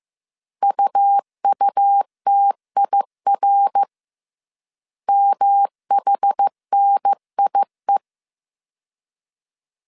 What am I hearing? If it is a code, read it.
Morse code, heard as a keyed tone, about 15 wpm, 800 Hz. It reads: UUTIR MHNIE